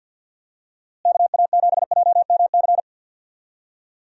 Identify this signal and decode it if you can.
Morse code, heard as a keyed tone, about 50 wpm, 700 Hz. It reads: KA7JMC
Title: USA Call Signs - 6 Characters - Repeated 2x - 50wpm